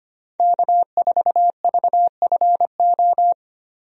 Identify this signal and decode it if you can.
Morse code, heard as a keyed tone, about 25 wpm, 690 Hz. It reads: K4VFO
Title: USA Call Signs - 5 Characters - 25wpm